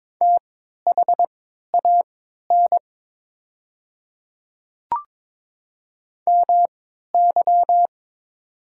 Morse code, heard as a keyed tone, about 21 wpm, spaced as for 12 wpm, 700 Hz. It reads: THAN MY
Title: 3x Character Spacing - Top 100 Words - 22wpm